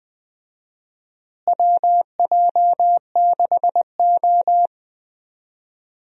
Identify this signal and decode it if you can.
Morse code, heard as a keyed tone, about 20 wpm, 700 Hz. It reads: WJ6O
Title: USA Call Signs - 4 Characters - Repeated 2x - 20wpm